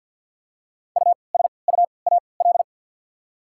Morse code, heard as a keyed tone, about 53 wpm, spaced as for 29 wpm, 700 Hz. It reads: USUAL
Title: 3x Character Spacing - Top 500 Words - 50wpm